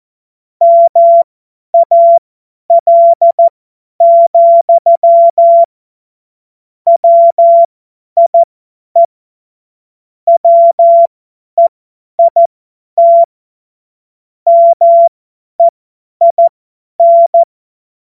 Morse code, heard as a keyed tone, about 14 wpm, spaced as for 10 wpm, 680 Hz. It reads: MAL, WIE WEIT MEIN
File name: Morsekurs / Witze / Witz 24.Oktober 2023 50 F70